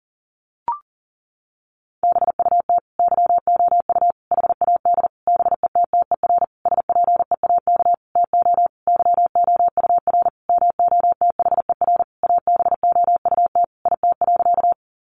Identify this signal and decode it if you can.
Morse code, heard as a keyed tone, about 40 wpm, 700 Hz. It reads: BUT YOU HAD BETTER SPEAK TO YOUR MOTHER ABOUT IT.
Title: Sentences from Top 1100 Words 40wpm